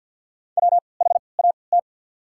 Morse code, heard as a keyed tone, about 54 wpm, spaced as for 29 wpm, 700 Hz. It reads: WHAT
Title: 3x Character Spacing - Top 100 Words - 50wpm